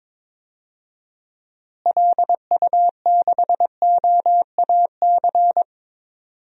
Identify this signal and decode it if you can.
Morse code, heard as a keyed tone, about 22 wpm, 700 Hz. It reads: LU6OAC
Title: DXCC-100 Call Signs - 6 Characters - Repeated 2x - 22wpm